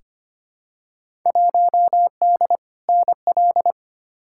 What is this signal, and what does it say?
Morse code, heard as a keyed tone, about 25 wpm, 700 Hz. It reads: 1D NL
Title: Field Day Exchanges Encore - 25wpm